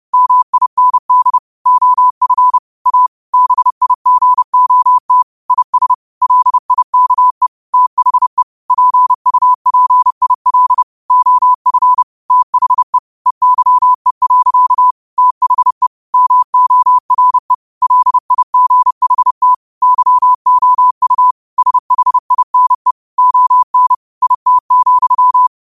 Morse code, heard as a keyed tone, about 30 wpm, 1000 Hz. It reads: MIND OF A BIGOT IS LIKE THE PUPIL OF THE EYE. THE MORE LIGHT YOU SHINE ON IT,
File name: qotdfiles / 30wpm